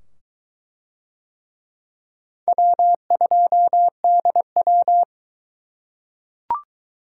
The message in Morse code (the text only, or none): W2DW